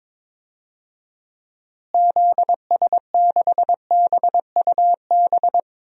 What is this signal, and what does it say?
Morse code, heard as a keyed tone, about 22 wpm, 700 Hz. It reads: ZS6BUB
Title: DXCC-100 Call Signs - 6 Characters - Repeated 2x - 22wpm